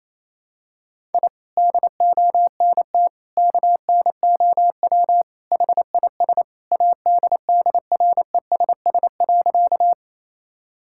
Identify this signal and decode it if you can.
Morse code, heard as a keyed tone, about 28 wpm, 700 Hz. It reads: I DONT KNOW HIS ADDRESS.
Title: Sentences from Top 1000 Words - Review Part 6 - 28wpm